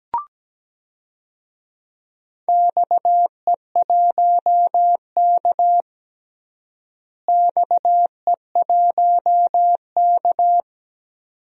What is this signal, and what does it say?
Morse code, heard as a keyed tone, about 17 wpm, 700 Hz. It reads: XE1K XE1K
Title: DXCC-100 Call Signs - 4 Characters - Repeated 2x - 17wpm